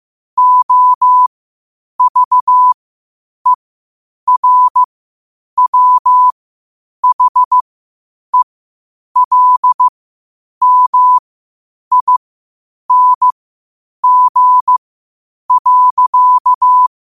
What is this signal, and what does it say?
Morse code, heard as a keyed tone, about 15 wpm, spaced as for 8 wpm, 1000 Hz. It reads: OVERWHELMING.
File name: qotdfiles / 8wpm